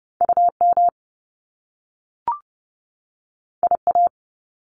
Morse code, heard as a keyed tone, about 30 wpm, 700 Hz. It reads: UM IA
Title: Top Two Letter Combinations in Words 30wpm